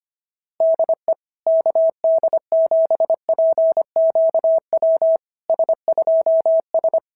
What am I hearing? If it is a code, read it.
Morse code, heard as a keyed tone, about 25 wpm, 650 Hz. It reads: DE KD7PQW S2S
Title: SOTA QSO Practice 25wpm